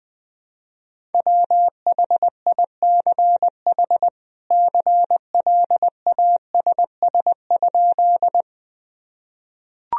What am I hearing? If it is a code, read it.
Morse code, heard as a keyed tone, about 20 wpm, 700 Hz. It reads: WHICH CLASS?
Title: Sentences from Top 1000 Words - Review Part 2 - 20wpm